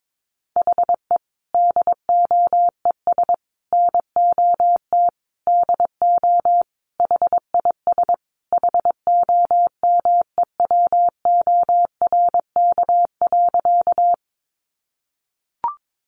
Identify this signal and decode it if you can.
Morse code, heard as a keyed tone, about 22 wpm, 700 Hz. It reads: HE DOES NOT DO HIS HOMEWORK.
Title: Additional Easy Phrases 22wpm